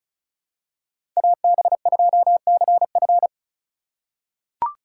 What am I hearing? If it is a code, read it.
Morse code, heard as a keyed tone, about 36 wpm, 690 Hz. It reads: AB2CF
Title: USA Call Signs - 5 Characters - 3x Character Spacing - 35wpm